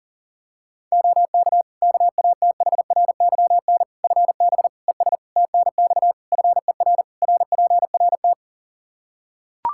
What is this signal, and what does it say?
Morse code, heard as a keyed tone, about 40 wpm, 700 Hz. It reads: OK KATHRYN FB ES TNX FER RPRT
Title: K1USN SST QSO Practice 40wpm